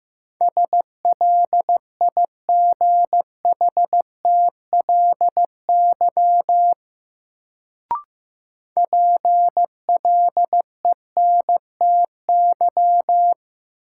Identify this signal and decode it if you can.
Morse code, heard as a keyed tone, about 15 wpm, 700 Hz. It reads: SLIGHTLY PLENTY
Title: Sentences from Top 1300 Words - New Word Drill - 15wpm